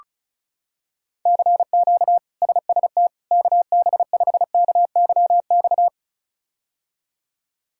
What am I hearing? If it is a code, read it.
Morse code, heard as a keyed tone, about 34 wpm, 700 Hz. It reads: CQ SST KB5KYX
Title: K1USN SST QSO Practice - 35wpm